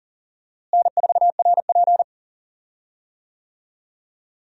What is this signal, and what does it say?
Morse code, heard as a keyed tone, about 40 wpm, 700 Hz. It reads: N4RP
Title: USA Call Signs - 4 Characters - Repeated 3x - 40wpm